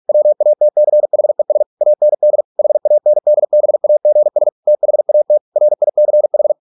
Morse code, heard as a keyed tone, about 46 wpm, 600 Hz. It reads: WATCHES AND HANDBAGS THAT RICH